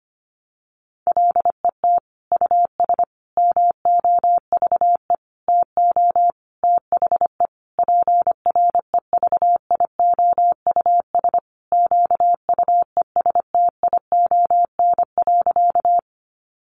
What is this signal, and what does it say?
Morse code, heard as a keyed tone, about 25 wpm, 700 Hz. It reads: LET US MOVE TO THE PREVIOUS QUESTION.
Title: Sentences from Top 1500 Words 25wpm